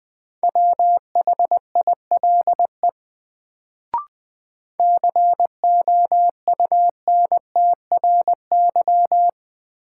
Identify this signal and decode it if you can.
Morse code, heard as a keyed tone, about 20 wpm, 700 Hz. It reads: WHILE COUNTRY
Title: Sentences from Top 300 Words - New Word Drill - 20wpm